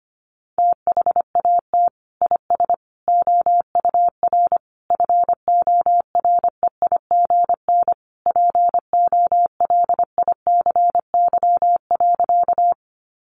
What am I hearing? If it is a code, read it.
Morse code, heard as a keyed tone, about 25 wpm, 700 Hz. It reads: THAT IS OUR FOREIGN POLICY.
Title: Sentences from Top 1400 Words 25wpm